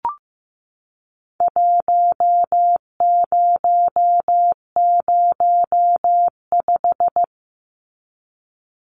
Morse code, heard as a keyed tone, about 15 wpm, 700 Hz. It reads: 1005
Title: Numbers - Contest Style - 4-Digit 15wpm